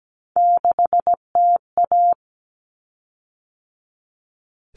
Morse code, heard as a keyed tone, about 17 wpm, 700 Hz. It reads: 6TA
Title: Common Cut Numbers - 3-Digit 17wpm